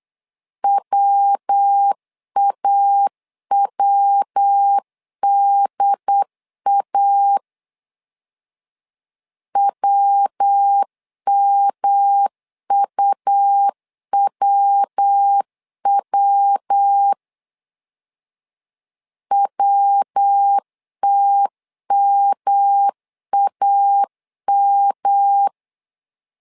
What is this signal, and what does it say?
Morse code, heard as a keyed tone, about 8 wpm, 800 Hz. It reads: WAWDA WMUWW WTMAM
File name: LEZIONI / 3 MWRO / 3 MW 8wpm 2501